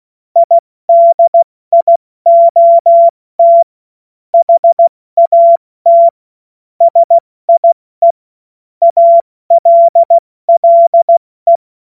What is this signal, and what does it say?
Morse code, heard as a keyed tone, about 16 wpm, 680 Hz. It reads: IDIOT HAT SIE ALLE
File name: Morsekurs / Witze / Witz 18. Mai 2021, 70 F80